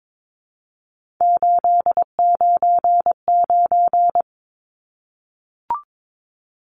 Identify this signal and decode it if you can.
Morse code, heard as a keyed tone, about 22 wpm, 700 Hz. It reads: 899
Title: Numbers - 3-Digit 22wpm